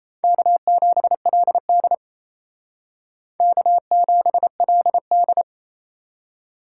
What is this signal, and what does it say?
Morse code, heard as a keyed tone, about 35 wpm, 700 Hz. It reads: K7LD K7LD